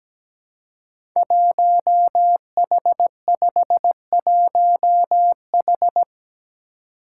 Morse code, heard as a keyed tone, about 17 wpm, 700 Hz. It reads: 1H51H
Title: Five Challenging Letter-Number Combinations 17wpm